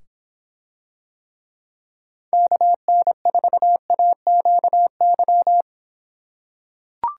Morse code, heard as a keyed tone, about 26 wpm, 700 Hz. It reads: KN4AQY